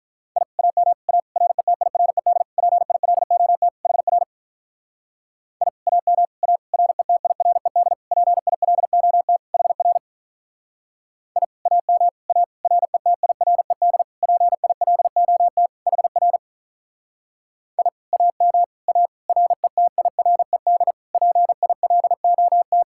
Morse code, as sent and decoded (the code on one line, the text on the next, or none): .. / .- -- / .- / .-. - .. .-. -.. / .--. .. .-.. --- - / .... .-. / .. / .- -- / .- / .-. - .. .-. -.. / .--. .. .-.. --- - / .... .-. / .. / .- -- / .- / .-. - .. .-. -.. / .--. .. .-.. --- - / .... .-. / .. / .- -- / .- / .-. - .. .-. -.. / .--. .. .-.. --- -
I AM A RETIRED PILOT HR I AM A RETIRED PILOT HR I AM A RETIRED PILOT HR I AM A RETIRED PILOT